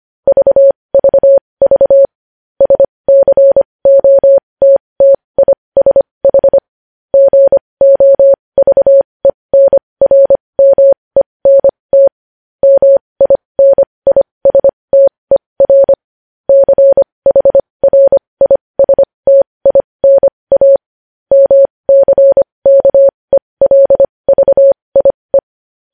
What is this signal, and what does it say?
Morse code, heard as a keyed tone, about 25 wpm, spaced as for 20 wpm, 550 Hz. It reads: VVV SCOTTISH GOVENRMENT MINISTER CHRISTINA MCKELVIE
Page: news-headlines-in-morse-code-at-20-wpm